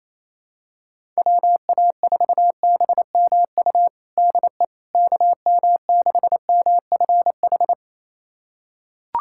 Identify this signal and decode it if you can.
Morse code, heard as a keyed tone, about 28 wpm, 700 Hz. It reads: WA4BMU DE KM6MFH
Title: USA Callsign DE Callsign 28wpm